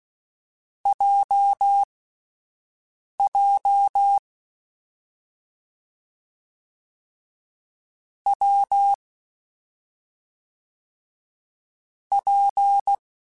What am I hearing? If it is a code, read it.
Morse code, heard as a keyed tone, about 16 wpm, spaced as for 5 wpm, 800 Hz. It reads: JJ W P
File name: Difficult Combos / Difficult Combos-mp3s / WPJ WPJ1 @5